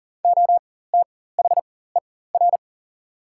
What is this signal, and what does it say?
Morse code, heard as a keyed tone, about 40 wpm, spaced as for 19 wpm, 700 Hz. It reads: OTHER
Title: 4x Character Spacing - Top 100 Words - 40wpm